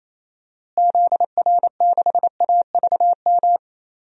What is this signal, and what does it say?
Morse code, heard as a keyed tone, about 28 wpm, 700 Hz. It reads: ZR6AVM
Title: DXCC-100 Call Signs - 6 Characters - Repeated 3x - 28wpm